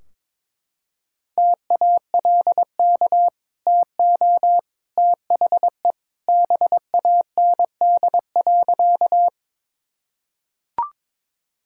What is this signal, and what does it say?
Morse code, heard as a keyed tone, about 21 wpm, 700 Hz. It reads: TALK TO THE BAND.
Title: Sentences from Top 1600 Words 22wpm